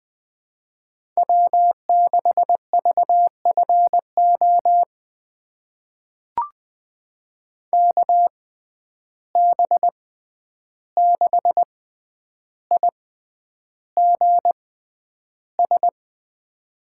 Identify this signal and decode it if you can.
Morse code, heard as a keyed tone, about 20 wpm, spaced as for 10 wpm, 700 Hz. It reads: W6VFO KB6IGS
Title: USA Call Signs - 6x Character Spacing - 20wpm